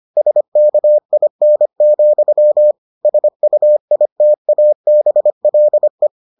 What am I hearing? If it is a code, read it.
Morse code, heard as a keyed tone, about 25 wpm, 600 Hz. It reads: SKIN, SUITABLE